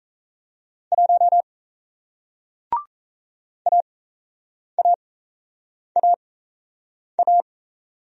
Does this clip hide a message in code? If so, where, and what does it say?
Yes: Morse code from 0.9 to 7.4 s, heard as a keyed tone, about 38 wpm, spaced as for 8 wpm, 700 Hz. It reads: 1 AAAA